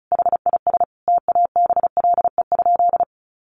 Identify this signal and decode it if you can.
Morse code, heard as a keyed tone, about 34 wpm, 700 Hz. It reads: HIS TABLE?
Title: Sentences from Top 1500 Words 35wpm